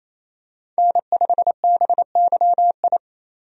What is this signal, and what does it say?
Morse code, heard as a keyed tone, about 28 wpm, 700 Hz. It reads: N5BYI
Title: USA Call Signs - 5 Characters - 28wpm